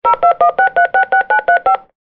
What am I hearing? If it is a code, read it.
DTMF, heard as telephone touch tones, tones of about 170 ms. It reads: 0216366935
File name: Phone-dialing